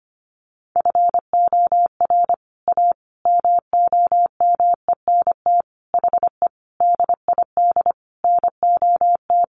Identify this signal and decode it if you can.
Morse code, heard as a keyed tone, about 25 wpm, 700 Hz. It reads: FOR A MOMENT HE DID NOT
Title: Sentences from Top 1100 Words 25wpm